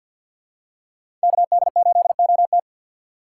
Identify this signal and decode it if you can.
Morse code, heard as a keyed tone, about 52 wpm, 700 Hz. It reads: KD8OT
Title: USA Call Signs - 2x Character Spacing - 50wpm